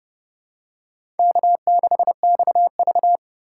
Morse code, heard as a keyed tone, about 32 wpm, 700 Hz. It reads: K6XV
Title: USA Call Signs - 4 Characters - 2x Character Spacing - 30wpm